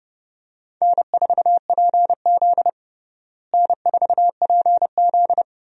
Morse code, heard as a keyed tone, about 31 wpm, 700 Hz. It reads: N4PZ N4PZ
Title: USA Call Signs - 4 Characters - Repeated 2x - 30wpm